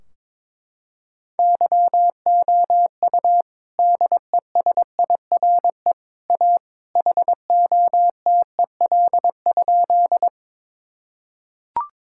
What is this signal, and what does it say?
Morse code, heard as a keyed tone, about 22 wpm, 700 Hz. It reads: YOU DESIRE A HOTEL?
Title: Sentences from Top 1600 Words 22wpm